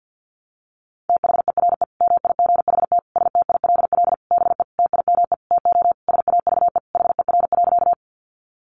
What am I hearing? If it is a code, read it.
Morse code, heard as a keyed tone, about 50 wpm, 700 Hz. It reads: THERE MIGHT STILL BE TIME TO SAVE HER.
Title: Sentences from Top 1100 Words 50wpm